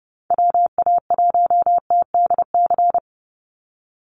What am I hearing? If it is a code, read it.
Morse code, heard as a keyed tone, about 30 wpm, 700 Hz. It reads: WA1TDC
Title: USA Call Signs 30wpm